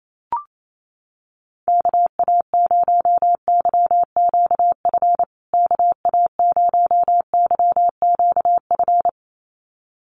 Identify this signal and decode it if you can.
Morse code, heard as a keyed tone, about 27 wpm, 700 Hz. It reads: KA0YQF KA0YQF
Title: USA Call Signs Repeated 2x 28wpm